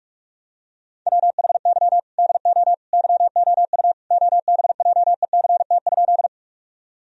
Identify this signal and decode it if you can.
Morse code, heard as a keyed tone, about 46 wpm, 700 Hz. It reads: WHY DO YOU OBJECT?
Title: Sentences from Top 1000 Words - Review Part 3 - 45wpm